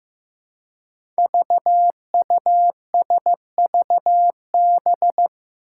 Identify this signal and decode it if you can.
Morse code, heard as a keyed tone, about 15 wpm, 700 Hz. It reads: VUSVB
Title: Five Challenging Letter-Number Combinations 15wpm